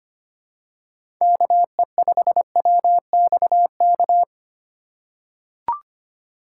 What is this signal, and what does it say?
Morse code, heard as a keyed tone, about 25 wpm, 700 Hz. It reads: KE5WXK
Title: USA Call Signs - 6 Characters - 4x Character Spacing - 25wpm